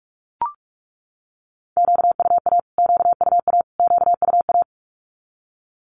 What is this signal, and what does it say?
Morse code, heard as a keyed tone, about 44 wpm, 700 Hz. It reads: QUA QUA QUA
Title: Q-Codes Repeated 3x 45wpm